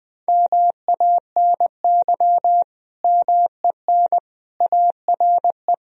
Morse code, heard as a keyed tone, about 20 wpm, 700 Hz. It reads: MANY MEN ARE
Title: Sentences from Top 1000 Words - Review Part 3 - 20wpm